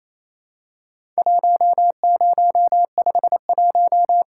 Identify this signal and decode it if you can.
Morse code, heard as a keyed tone, about 27 wpm, 700 Hz. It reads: 1051
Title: Worked All Europe DX Contest - QTC Exchange - 28wpm